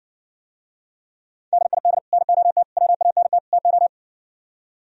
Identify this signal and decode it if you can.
Morse code, heard as a keyed tone, about 59 wpm, 700 Hz. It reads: DID NOT WANT TO